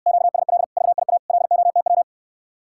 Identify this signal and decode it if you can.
Morse code, heard as a keyed tone, about 69 wpm, 700 Hz. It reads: QSL VIA LOTW